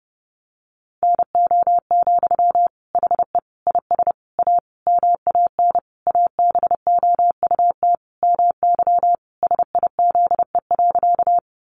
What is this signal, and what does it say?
Morse code, heard as a keyed tone, about 30 wpm, 700 Hz. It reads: NO, HE IS A MAN ABOUT MY SIZE.
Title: Sentences from Top 400 Words 30wpm